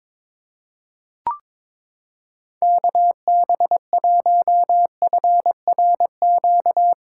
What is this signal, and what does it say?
Morse code, heard as a keyed tone, about 22 wpm, 700 Hz. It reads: KB1FRQ
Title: USA Callsign DE Callsign - 6 Characters - 22wpm